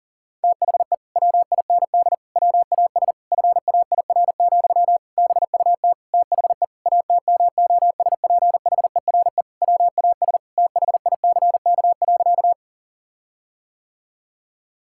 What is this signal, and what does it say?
Morse code, heard as a keyed tone, about 40 wpm, 700 Hz. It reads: THE WIND WAS FAIR, BUT THE ATMOSPHERE WAS THICK.
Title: Sentences from Top 1600 Words 40wpm